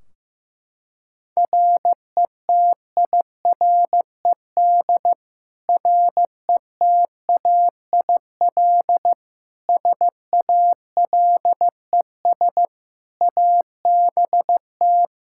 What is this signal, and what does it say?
Morse code, heard as a keyed tone, about 16 wpm, 700 Hz. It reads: RETIRED RETAIL SALES ABT
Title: Practice QSO - Exchange 3 - More Info - 15wpm